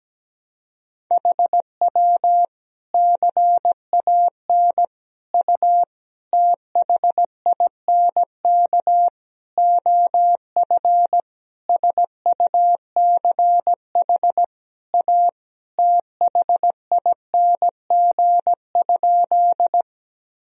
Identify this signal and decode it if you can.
Morse code, heard as a keyed tone, about 17 wpm, 700 Hz. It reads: HW CAN U THINK OF SUCH A THING?
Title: Sentences from Top 200 Words and Abbreviations 17wpm